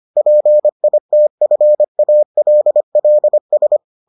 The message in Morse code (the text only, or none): PITFALLS